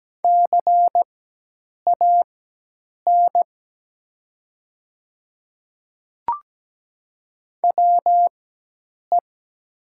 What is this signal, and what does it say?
Morse code, heard as a keyed tone, about 17 wpm, spaced as for 8 wpm, 700 Hz. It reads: CAN WE
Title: 4x Character Spacing - Top 100 Words - 17wpm